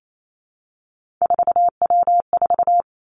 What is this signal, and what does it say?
Morse code, heard as a keyed tone, about 28 wpm, 700 Hz. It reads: 4W4
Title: Three Challenging Letter-Number Combinations 28wpm